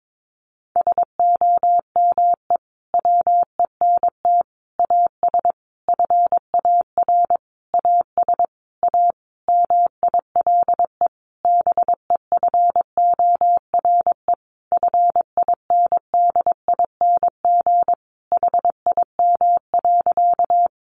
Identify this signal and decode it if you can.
Morse code, heard as a keyed tone, about 22 wpm, 700 Hz. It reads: SOME WENT AS FAR AS A MILE BEFORE FINDING HIM.
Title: Sentences from Top 1100 Words 22wpm